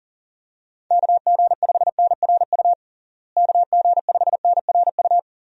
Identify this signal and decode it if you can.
Morse code, heard as a keyed tone, about 37 wpm, 700 Hz. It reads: KG5NRU KG5NRU
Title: USA Call Signs - 6 Characters - Repeated 3x - 40wpm